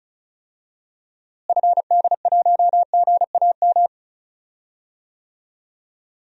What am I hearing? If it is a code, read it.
Morse code, heard as a keyed tone, about 34 wpm, 700 Hz. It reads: FD1GAM
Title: DXCC-100 Call Signs - 6 Characters - Repeated 2x - 35wpm